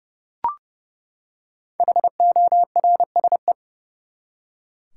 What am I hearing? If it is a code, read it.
Morse code, heard as a keyed tone, about 30 wpm, 700 Hz. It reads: HORSE
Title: Five Letter Words - 30wpm